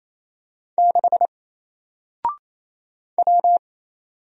Morse code, heard as a keyed tone, about 28 wpm, 700 Hz. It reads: 6 W
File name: Single-Letter-Number-28wpm